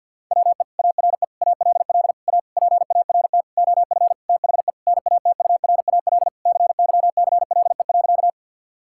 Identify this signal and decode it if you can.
Morse code, heard as a keyed tone, about 49 wpm, 700 Hz. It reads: WE ARE ALL A PART OF THE NATURAL CYCLE.
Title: Sentences from Top 1000 Words - Review Part 6 - 50wpm